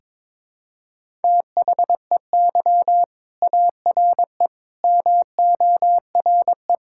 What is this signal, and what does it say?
Morse code, heard as a keyed tone, about 22 wpm, 700 Hz. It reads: THEY ARE MORE
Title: Sentences from Top 1000 Words - Review Part 5 - 22wpm